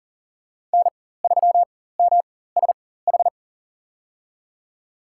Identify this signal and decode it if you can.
Morse code, heard as a keyed tone, about 40 wpm, spaced as for 19 wpm, 700 Hz. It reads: N3MSH
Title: USA Call Signs - 5 Characters - 4x Character Spacing - 40wpm